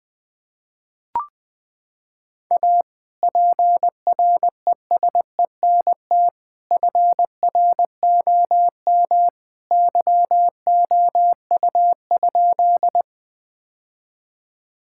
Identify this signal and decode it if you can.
Morse code, heard as a keyed tone, about 21 wpm, 700 Hz. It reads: A PRESENT FROM YOU?
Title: Sentences from Top 1000 Words - Review Part 5 - 20wpm